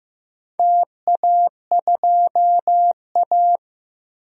Morse code, heard as a keyed tone, about 15 wpm, 700 Hz. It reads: TA2A
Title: DXCC-100 Call Signs - 4 Characters - 15wpm